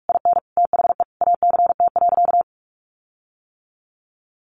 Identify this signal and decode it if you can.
Morse code, heard as a keyed tone, about 44 wpm, 700 Hz. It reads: IN THE ACT.
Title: Sentences from Top 1300 Words 45wpm